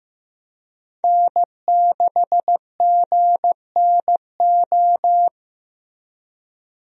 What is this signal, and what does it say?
Morse code, heard as a keyed tone, about 15 wpm, 700 Hz. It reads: N6GNO
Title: USA Call Signs - 5 Characters - Repeated 3x - 15wpm